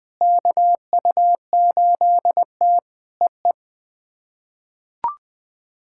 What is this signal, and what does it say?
Morse code, heard as a keyed tone, about 20 wpm, 700 Hz. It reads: KU8T EE
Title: POTA QSO Practice - 20wpm